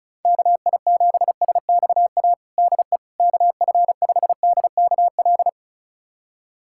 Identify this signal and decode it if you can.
Morse code, heard as a keyed tone, about 34 wpm, 700 Hz. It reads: KI7SXA DE KF5DKL
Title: USA Callsign DE Callsign - 6 Characters - 35wpm